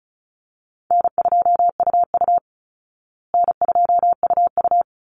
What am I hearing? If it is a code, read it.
Morse code, heard as a keyed tone, about 34 wpm, 700 Hz. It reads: N2UU N2UU
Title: Easy Call Signs Repeated 2x - 35wpm